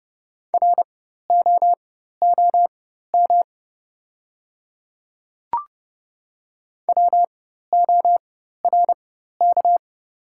Morse code, heard as a keyed tone, about 30 wpm, spaced as for 14 wpm, 690 Hz. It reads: ROOM WORK
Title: 4x Character Spacing - Top 500 Words - 30wpm